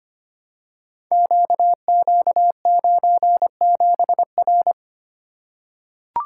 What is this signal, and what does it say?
Morse code, heard as a keyed tone, about 25 wpm, 700 Hz. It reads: QQ97R
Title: Code Groups - 5 Letter:Numbers - 25wpm